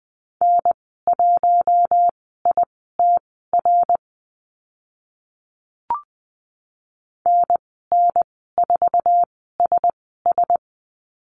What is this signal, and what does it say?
Morse code, heard as a keyed tone, about 20 wpm, spaced as for 15 wpm, 700 Hz. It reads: N1ITR NN4SS
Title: 055 Lesson 10g - Letter r - Call Signs 20wpm